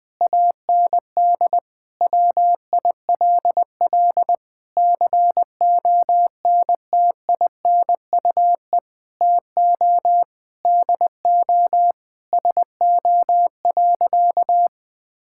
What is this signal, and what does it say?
Morse code, heard as a keyed tone, about 20 wpm, 700 Hz. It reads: AND WILL CONTINUE TO DO SO.
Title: Sentences with 6-Word Phrases - 20wpm